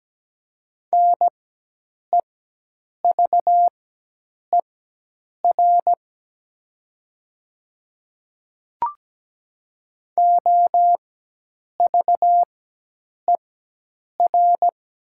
Morse code, heard as a keyed tone, about 17 wpm, spaced as for 8 wpm, 700 Hz. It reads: NEVER OVER
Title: 4x Character Spacing - Top 300 Words - 17wpm